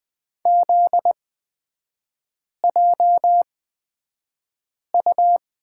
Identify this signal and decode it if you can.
Morse code, heard as a keyed tone, about 20 wpm, spaced as for 5 wpm, 700 Hz. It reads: ZJU
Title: 243 Lesson 39e - Letter z - Three Characters 20wpm